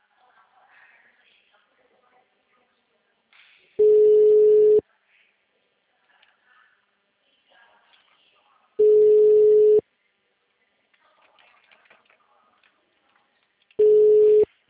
Ringback tone (Europe, 425 Hz).